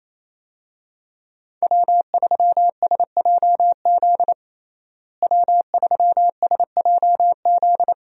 Morse code, heard as a keyed tone, about 28 wpm, 700 Hz. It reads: W3SJZ W3SJZ
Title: USA Call Signs - 5 Characters - Repeated 2x - 28wpm